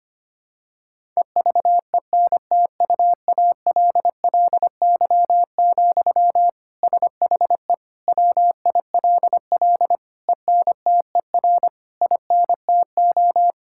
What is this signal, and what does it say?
Morse code, heard as a keyed tone, about 25 wpm, 700 Hz. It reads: EVENTUALLY, SHE WILL ENTER INTO
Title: Sentences from Top 1000 Words - Review Part 6 - 25wpm